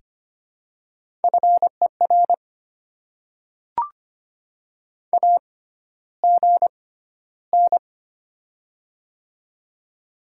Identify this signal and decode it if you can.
Morse code, heard as a keyed tone, about 25 wpm, spaced as for 13 wpm, 700 Hz. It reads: FER AGN
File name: Top 20 QSO Elements - 6x Character Spacing - 25wpm